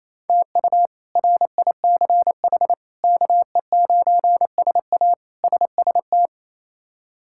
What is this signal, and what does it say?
Morse code, heard as a keyed tone, about 28 wpm, 700 Hz. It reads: TU RICH KE9SA SST
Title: K1USN SST QSO Practice - 28wpm